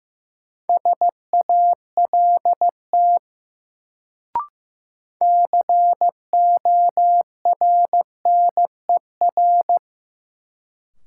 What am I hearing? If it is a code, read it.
Morse code, heard as a keyed tone, about 15 wpm, 700 Hz. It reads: SALT CORNER
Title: Sets of 5 Words - New Word Drill - 15wpm